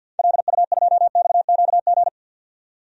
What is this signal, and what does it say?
Morse code, heard as a keyed tone, about 49 wpm, 700 Hz. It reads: LU2XQG
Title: DXCC-100 Call Signs - 6 Characters - 50wpm